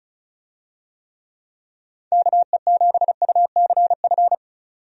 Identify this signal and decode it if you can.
Morse code, heard as a keyed tone, about 34 wpm, 700 Hz. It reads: KE7UCF
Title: USA Call Signs - 6 Characters - 35wpm